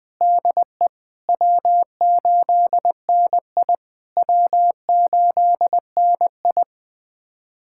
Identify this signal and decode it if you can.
Morse code, heard as a keyed tone, about 20 wpm, 700 Hz. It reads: DE W8NI W8NI
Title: POTA QSO Practice - 20wpm